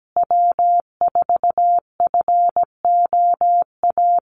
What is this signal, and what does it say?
Morse code, heard as a keyed tone, about 17 wpm, 700 Hz. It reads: W4FOA
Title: Call Signs Repeated 3x - 17wpm